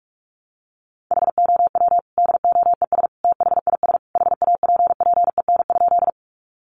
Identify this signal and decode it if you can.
Morse code, heard as a keyed tone, about 44 wpm, 700 Hz. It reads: HOW DOES THIS HAPPEN?
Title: Sentences from Top 400 Words 45wpm